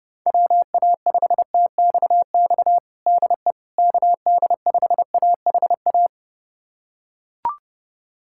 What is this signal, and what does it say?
Morse code, heard as a keyed tone, about 32 wpm, 700 Hz. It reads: WA5TXX DE KD5AHA
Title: USA Callsign DE Callsign - 6 Characters - 30wpm